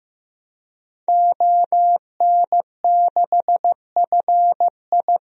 Morse code, heard as a keyed tone, about 15 wpm, 700 Hz. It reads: ON6FI
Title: DXCC-100 Call Signs - 5 Characters - Repeated 3x - 15wpm